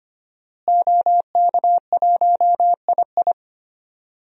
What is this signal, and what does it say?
Morse code, heard as a keyed tone, about 25 wpm, 700 Hz. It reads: OK1II
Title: DXCC-100 Call Signs - 5 Characters - Repeated 3x - 25wpm